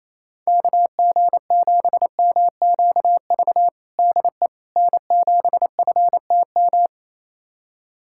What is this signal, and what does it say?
Morse code, heard as a keyed tone, about 28 wpm, 700 Hz. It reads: KG7MQV DE N7FTM
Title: USA Callsign DE Callsign 28wpm